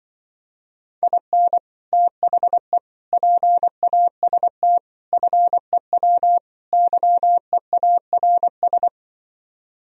Morse code, heard as a keyed tone, about 24 wpm, 700 Hz. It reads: IN THE PAST FEW YEARS
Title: Speed-Racing - Sets of 5 Words 20wpm